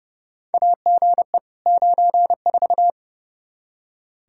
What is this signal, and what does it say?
Morse code, heard as a keyed tone, about 31 wpm, 700 Hz. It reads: AGE 94
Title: K1USN SST QSO Practice 30wpm